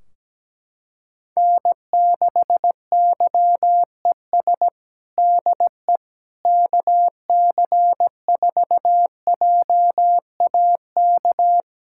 Morse code, heard as a keyed tone, about 17 wpm, 700 Hz. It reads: N6YES DE KC4JAK
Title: K1USN SST QSO Practice 17wpm